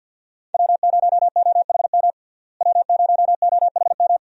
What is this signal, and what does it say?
Morse code, heard as a keyed tone, about 48 wpm, 700 Hz. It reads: W0OHM W0OHM
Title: USA Call Signs - 5 Characters - Repeated 2x - 50wpm